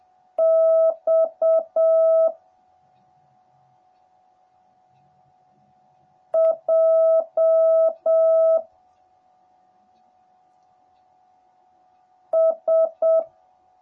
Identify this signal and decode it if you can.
Morse code, heard as a keyed tone, about 7 wpm, 650 Hz. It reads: X J S